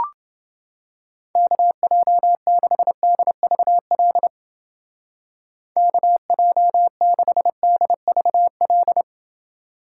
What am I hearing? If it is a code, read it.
Morse code, heard as a keyed tone, about 29 wpm, 700 Hz. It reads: KJ6DVL KJ6DVL